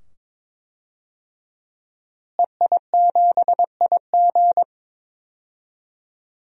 Morse code, heard as a keyed tone, about 22 wpm, 700 Hz. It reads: EI7IG
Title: DXCC-100 Call Signs - 5 Characters - Repeated 2x - 22wpm